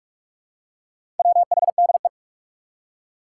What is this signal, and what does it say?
Morse code, heard as a keyed tone, about 44 wpm, 700 Hz. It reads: WHDE
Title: Four Challenging Letter-Number Combinations 45wpm